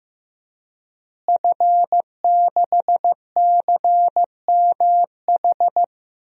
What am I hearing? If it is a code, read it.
Morse code, heard as a keyed tone, about 16 wpm, 700 Hz. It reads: F6CMH
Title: DXCC-100 Call Signs - 5 Characters - Repeated 3x - 15wpm